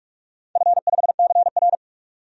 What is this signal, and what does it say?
Morse code, heard as a keyed tone, about 46 wpm, 700 Hz. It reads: F5CR
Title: DXCC-100 Call Signs - 4 Characters - 45wpm